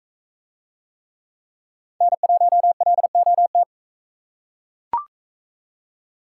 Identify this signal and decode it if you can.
Morse code, heard as a keyed tone, about 41 wpm, 700 Hz. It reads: N1LOT